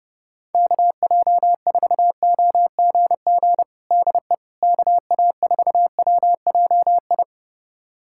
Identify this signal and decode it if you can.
Morse code, heard as a keyed tone, about 30 wpm, 700 Hz. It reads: KJ4OGG DE KA4WJI
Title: USA Callsign DE Callsign - 6 Characters - 30wpm